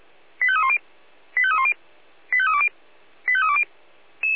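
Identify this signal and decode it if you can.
Five-tone selective call (ZVEI), heard as quick runs of five tones, tones of about 70 ms. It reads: 85310 85310 85320 85320
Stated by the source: ZVEI five-tone selective call 85356, received by SDR.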